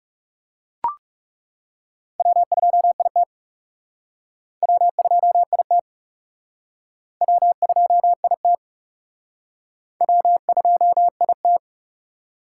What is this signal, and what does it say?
Morse code, heard as a keyed tone, about 37 wpm, 700 Hz. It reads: W2IT W2IT W2IT W2IT
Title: Speed Racing - USA 1x2 Call Signs 30wpm